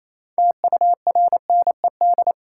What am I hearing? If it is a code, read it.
Morse code, heard as a keyed tone, about 28 wpm, 700 Hz. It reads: TURNED
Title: Sets of 4 Words - New Word Drill - 28wpm